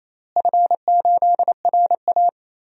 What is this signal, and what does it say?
Morse code, heard as a keyed tone, about 28 wpm, 700 Hz. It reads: F8RA
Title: DXCC-100 Call Signs - 4 Characters - Repeated 3x - 28wpm